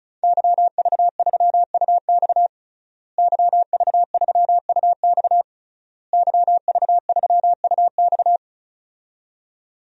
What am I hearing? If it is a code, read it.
Morse code, heard as a keyed tone, about 36 wpm, 700 Hz. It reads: YV3UX YV3UX YV3UX
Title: DXCC-100 Call Signs - 5 Characters - Repeated 3x - 35wpm